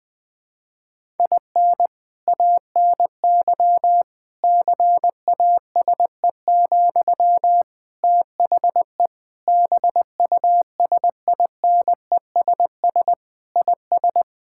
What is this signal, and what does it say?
Morse code, heard as a keyed tone, about 20 wpm, 700 Hz. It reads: IN ANY CASE, THE BUSINESS IS
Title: Sentences from Top 1000 Words - Review Part 5 - 20wpm